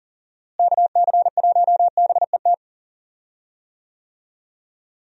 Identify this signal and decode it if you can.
Morse code, heard as a keyed tone, about 40 wpm, 700 Hz. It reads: KC1BET
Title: USA Call Signs - 6 Characters - Repeated 2x - 40wpm